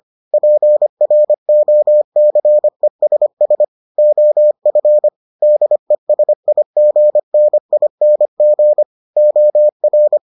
Morse code, heard as a keyed tone, about 25 wpm, 600 Hz. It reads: PROCESS OF DESIGNING OR